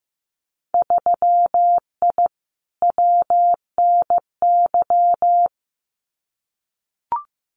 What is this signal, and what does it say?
Morse code, heard as a keyed tone, about 15 wpm, 700 Hz. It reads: 3I WNY
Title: Winter Field Day Exchanges 15wpm